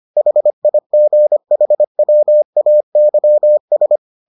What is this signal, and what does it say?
Morse code, heard as a keyed tone, about 25 wpm, 600 Hz. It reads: HIGHWAYS